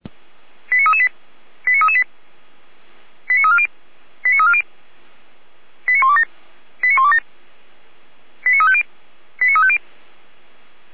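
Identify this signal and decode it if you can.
Five-tone selective call (ZVEI), heard as quick runs of five tones, tones of about 70 ms. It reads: 89338 89338 89340 89340 89127 89127 89350 89350
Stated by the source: ZVEI five-tone call received by SDR.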